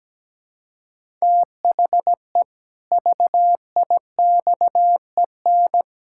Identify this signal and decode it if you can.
Morse code, heard as a keyed tone, about 17 wpm, 700 Hz. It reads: THE VIXEN
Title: Pangram Challenge v2 - 17wpm